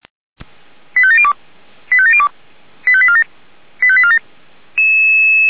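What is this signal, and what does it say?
Five-tone selective call (ZVEI), heard as quick runs of five tones, tones of about 70 ms. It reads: 85702 85702 85847 85847
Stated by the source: ZVEI five-tone selective call received with an SDR.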